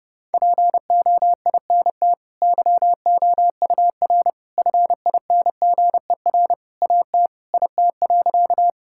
Morse code, heard as a keyed tone, about 32 wpm, 700 Hz. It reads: POINT YOUR FINGER AT IT.
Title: Sentences from Top 1600 Words 30wpm